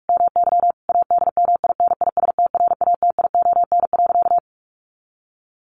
Morse code, heard as a keyed tone, about 44 wpm, 700 Hz. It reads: MY ADMINISTRATION.
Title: Sentences from Top 1300 Words 45wpm